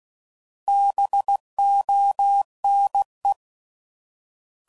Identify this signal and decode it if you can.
Morse code, heard as a keyed tone, about 16 wpm, 800 Hz. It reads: BONE